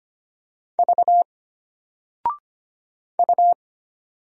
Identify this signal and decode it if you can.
Morse code, heard as a keyed tone, about 25 wpm, 700 Hz. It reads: V U
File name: Single-Letter-Number-25wpm